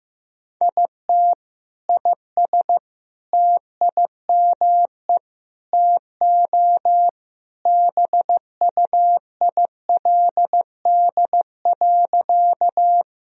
Morse code, heard as a keyed tone, about 15 wpm, 700 Hz. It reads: IT IS TIME TO BUILD.
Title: Sentences from Top 1000 Words - Review Part 1 - 15wpm